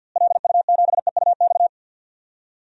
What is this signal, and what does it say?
Morse code, heard as a keyed tone, about 51 wpm, 700 Hz. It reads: LU7EUX
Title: DXCC-100 Call Signs - 6 Characters - Repeated 3x - 50wpm